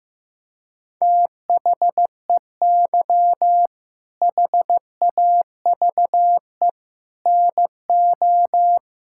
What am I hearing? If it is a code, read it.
Morse code, heard as a keyed tone, about 16 wpm, 700 Hz. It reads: THEY HAVE NO
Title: Sentences from Top 1600 Words 15wpm